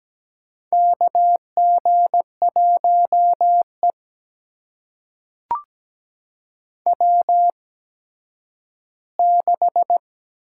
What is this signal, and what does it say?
Morse code, heard as a keyed tone, about 17 wpm, spaced as for 10 wpm, 700 Hz. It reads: KG1E W6
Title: USA Call Signs - 4 Characters - 8x Character Spacing - 17wpm